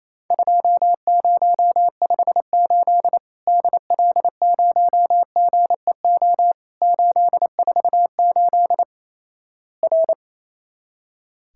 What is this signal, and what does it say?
Morse code, heard as a keyed tone, about 27 wpm, 700 Hz. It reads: 2058 DL0GEO 848 R
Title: Worked All Europe DX Contest - QTC Exchange - 28wpm